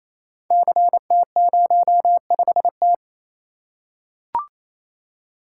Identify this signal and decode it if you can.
Morse code, heard as a keyed tone, about 28 wpm, 700 Hz. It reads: CT05T
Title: Code Groups - 5 Letter:Numbers - 28wpm